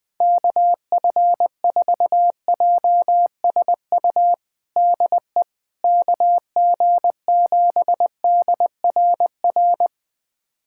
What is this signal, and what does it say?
Morse code, heard as a keyed tone, about 20 wpm, 700 Hz. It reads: KF4JSU DE KG7DRR
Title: USA Callsign DE Callsign 20wpm